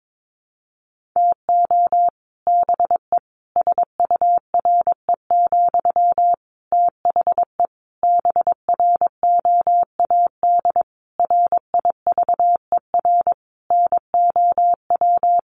Morse code, heard as a keyed tone, about 22 wpm, 700 Hz. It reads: TO BE SURE, THE BROAD RIVER NOW
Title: Sentences from Top 1000 Words 22wpm